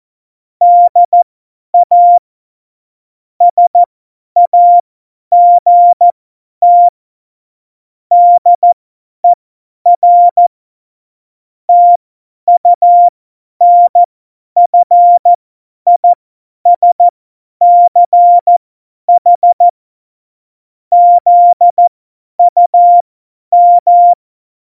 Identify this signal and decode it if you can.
Morse code, heard as a keyed tone, about 14 wpm, spaced as for 10 wpm, 700 Hz. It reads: DA SAGT DER TUNFISCH ZUM
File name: Morsekurs / Witze / Witz 13. November 2018, 50 F70